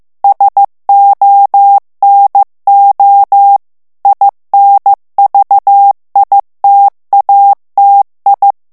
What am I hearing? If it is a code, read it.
Morse code, heard as a keyed tone, about 15 wpm, 800 Hz. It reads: SONO INVITATI
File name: ITALIEN texte clair 1 vitesse 14 mots minute